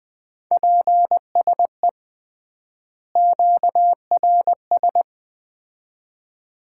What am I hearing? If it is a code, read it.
Morse code, heard as a keyed tone, about 21 wpm, 700 Hz. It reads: PSE QRS
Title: Common QSO Phrases - 3x Word-Spacing - 20wpm